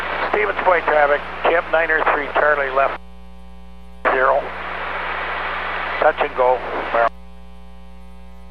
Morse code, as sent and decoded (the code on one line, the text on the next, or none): none